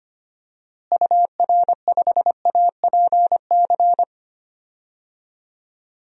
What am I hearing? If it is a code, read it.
Morse code, heard as a keyed tone, about 25 wpm, 700 Hz. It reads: UR5APC
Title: DXCC-100 Call Signs - 6 Characters - Repeated 2x - 25wpm